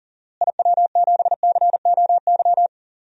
Morse code, heard as a keyed tone, about 37 wpm, 700 Hz. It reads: IW7COY
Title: DXCC-100 Call Signs - 6 Characters - Repeated 2x - 40wpm